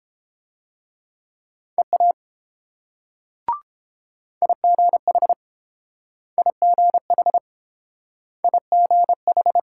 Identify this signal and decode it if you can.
Morse code, heard as a keyed tone, about 32 wpm, 700 Hz. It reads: EA IGH IGH IGH